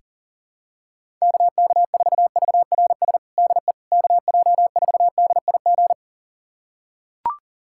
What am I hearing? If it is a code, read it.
Morse code, heard as a keyed tone, about 40 wpm, 700 Hz. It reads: KK4VRS DE KJ4DIG
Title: USA Callsign DE Callsign - 6 Characters - 40wpm